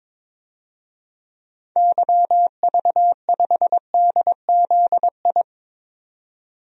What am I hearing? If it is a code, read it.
Morse code, heard as a keyed tone, about 22 wpm, 700 Hz. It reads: YV5DZI
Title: DXCC-100 Call Signs - 6 Characters - 22wpm